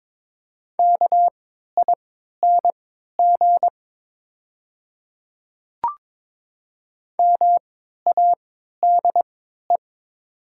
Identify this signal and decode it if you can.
Morse code, heard as a keyed tone, about 23 wpm, spaced as for 13 wpm, 700 Hz. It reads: KING MADE
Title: 3x Character Spacing - Top 500 Words - 22wpm